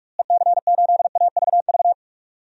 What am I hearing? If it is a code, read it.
Morse code, heard as a keyed tone, about 46 wpm, 700 Hz. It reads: EC8AVV
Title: DXCC-100 Call Signs - 6 Characters - Repeated 3x - 45wpm